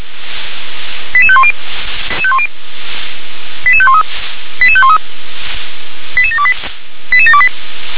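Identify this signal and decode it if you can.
Five-tone selective call (ZVEI), heard as quick runs of five tones, tones of about 70 ms. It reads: 88510 88513 88513 88728 88728
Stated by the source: ZVEI selective call received on SDR.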